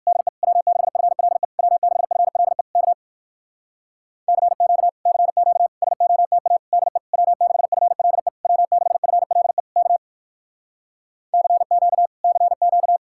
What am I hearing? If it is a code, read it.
Morse code, heard as a keyed tone, about 59 wpm, 700 Hz. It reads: DE W6FLE W6FLE K CQ CQ SOTA DE W6FLE W6FLE K CQ CQ